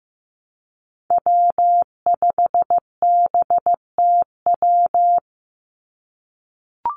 Morse code, heard as a keyed tone, about 15 wpm, 700 Hz. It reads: W5BTW
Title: USA Call Signs 15wpm